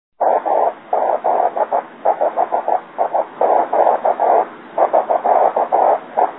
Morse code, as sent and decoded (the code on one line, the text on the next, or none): none